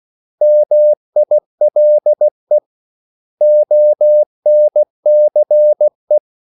Morse code, heard as a keyed tone, about 16 wpm, 600 Hz. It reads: MILE ONCE
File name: CW / Practice / 500MostCommonEnglishWords16